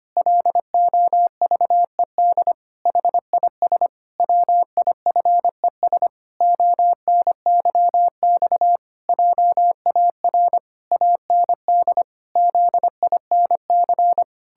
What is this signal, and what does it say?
Morse code, heard as a keyed tone, about 25 wpm, 700 Hz. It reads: LOVED HIS WIFES ONYX JAR AND ZINC
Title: Pangram Challenge v2 - 25wpm